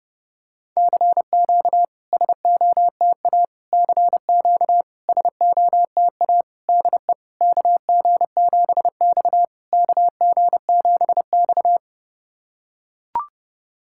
Morse code, heard as a keyed tone, about 28 wpm, 690 Hz. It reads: CQ SOTA CQ SOTA DE KG7X KG7X
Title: SOTA QSO Practice 30wpm